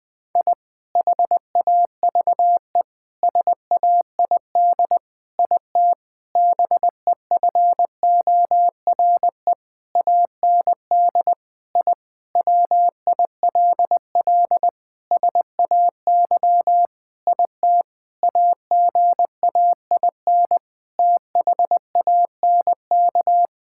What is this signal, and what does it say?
Morse code, heard as a keyed tone, about 20 wpm, 700 Hz. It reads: I HAVE SAID IT BEFORE AND I WILL SAY IT AGAIN THANK
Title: Sentences with 8-Word Phrases - Spoken Beginning - 20wpm